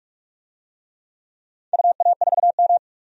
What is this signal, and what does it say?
Morse code, heard as a keyed tone, about 46 wpm, 700 Hz. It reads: UA4M
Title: DXCC-100 Call Signs - 4 Characters - 45wpm